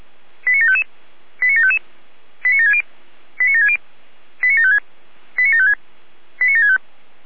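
Five-tone selective call (ZVEI), heard as quick runs of five tones, tones of about 80 ms. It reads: 89755 89755 89760 89760 89756 89756 89765
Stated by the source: ZVEI five-tone selective call received on SDR.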